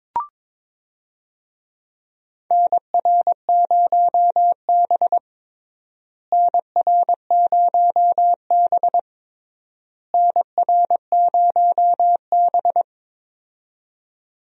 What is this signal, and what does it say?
Morse code, heard as a keyed tone, about 22 wpm, 700 Hz. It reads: NR0B NR0B NR0B
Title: USA Call Signs - 4 Characters - Repeated 3x - 22wpm